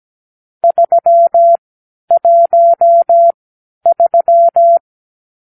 Morse code, heard as a keyed tone, about 17 wpm, spaced as for 10 wpm, 680 Hz. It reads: 313
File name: arrl / Review (1-5)